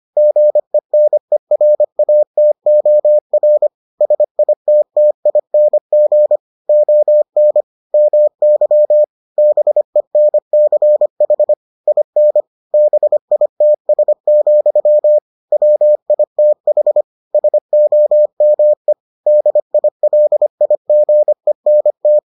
Morse code, heard as a keyed tone, about 25 wpm, 600 Hz. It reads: GENERATOR SITTING ON MY BENCH IN BITS, WITH SOME DILIGENT